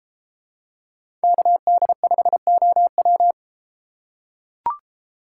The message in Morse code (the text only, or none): KD5OW